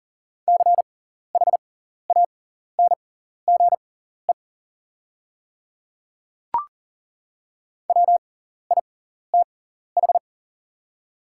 Morse code, heard as a keyed tone, about 40 wpm, spaced as for 14 wpm, 690 Hz. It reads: CHANGE WITH
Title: 6x Character Spacing - Top 200 Words - 40wpm